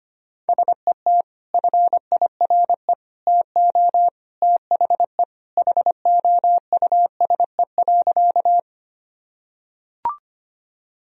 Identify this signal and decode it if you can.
Morse code, heard as a keyed tone, about 25 wpm, 700 Hz. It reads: SET FIRE TO THE HOUSE.
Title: Sentences from Top 1000 Words - Review Part 1 - 25wpm